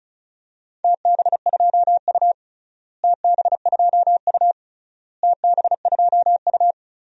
Morse code, heard as a keyed tone, about 36 wpm, 700 Hz. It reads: TB2U TB2U TB2U
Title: DXCC-100 Call Signs - 4 Characters - Repeated 3x - 35wpm